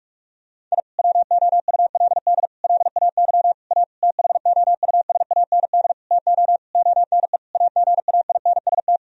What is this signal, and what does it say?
Morse code, heard as a keyed tone, about 47 wpm, 700 Hz. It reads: I WOULD LAY A THOUSAND TO ONE AGAINST
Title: Sentences from Top 1000 Words - Review Part 3 - 45wpm